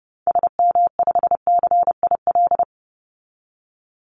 Morse code, heard as a keyed tone, about 30 wpm, 700 Hz. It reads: SM5CIL
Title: Call Signs - 30wpm